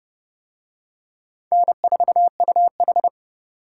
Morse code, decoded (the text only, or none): N4UH